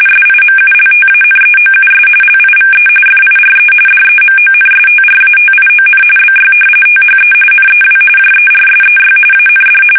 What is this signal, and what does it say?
RTTY, about 45.45 baud. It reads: SIMULTANEOUS LIVE FIRE DEMONSTRATION BY THE NEW JERSEY AND
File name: fsk850 / battleshipnj